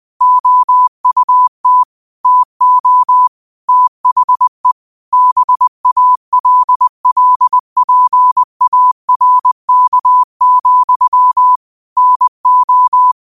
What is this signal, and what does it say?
Morse code, heard as a keyed tone, about 20 wpm, 1000 Hz. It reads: OUT TO THE BALLPARK, NO
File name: qotdfiles / 20wpm